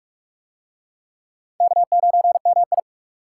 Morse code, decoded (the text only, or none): K9MI